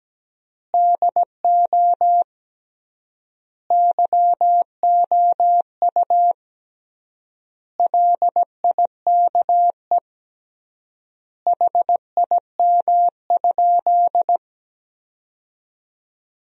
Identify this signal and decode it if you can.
Morse code, heard as a keyed tone, about 17 wpm, 700 Hz. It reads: DO YOU LIKE HIM?
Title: 3x Word-Spacing - Sentences from Top 100 Words - 17wpm